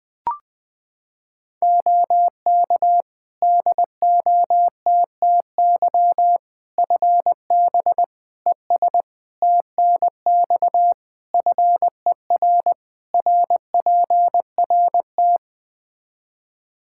Morse code, heard as a keyed tone, about 21 wpm, 700 Hz. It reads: OK DOTTY FB ES TNX FER RPRT
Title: K1USN SST QSO Practice 20wpm